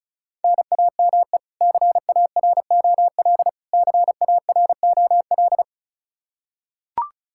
Morse code, heard as a keyed tone, about 36 wpm, 700 Hz. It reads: NAME CAROL CAROL
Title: Practice QSO - Exchange 1 - Beginning - 35wpm